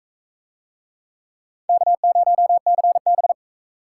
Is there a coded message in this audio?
Yes: Morse code from 1.7 to 3.3 s, heard as a keyed tone, about 39 wpm, 700 Hz. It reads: K0CB